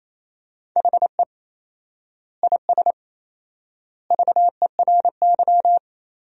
Morse code, heard as a keyed tone, about 28 wpm, 700 Hz. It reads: HE IS VERY
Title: 4x Word-Spacing - Sentences from Top 400 Words - 28wpm